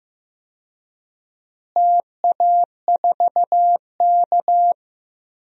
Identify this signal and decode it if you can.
Morse code, heard as a keyed tone, about 15 wpm, 700 Hz. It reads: TA4K
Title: DXCC-100 Call Signs - 4 Characters - Repeated 2x - 15wpm